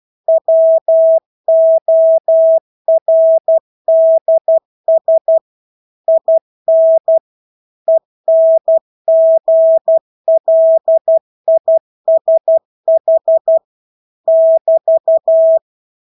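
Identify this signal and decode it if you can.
Morse code, heard as a keyed tone, about 12 wpm, 650 Hz. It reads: WORDS IN ENGLISH <BT>
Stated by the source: Morse practice of English words at 12 wpm.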